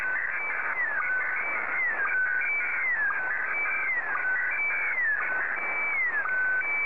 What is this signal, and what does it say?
SSTV (slow-scan television), Scottie DX.